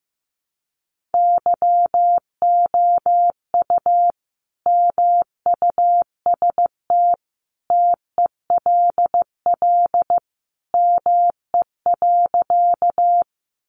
Morse code, heard as a keyed tone, about 15 wpm, 700 Hz. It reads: YOU MUST TELL ME.
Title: Sentences from Top 200 Words 15wpm